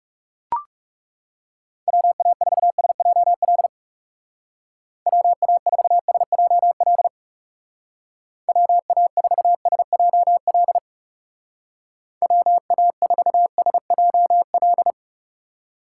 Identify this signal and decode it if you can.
Morse code, heard as a keyed tone, about 40 wpm, 700 Hz. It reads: WA4SJL WA4SJL WA4SJL WA4SJL